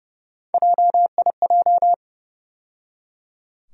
Morse code, heard as a keyed tone, about 30 wpm, 700 Hz. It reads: JIJ
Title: USA Call Sign Postfixes - 3 Letters 30wpm